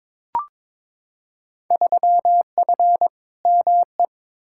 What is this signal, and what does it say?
Morse code, heard as a keyed tone, about 22 wpm, 700 Hz. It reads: 3F ME
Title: Field Day Exchanges - 22wpm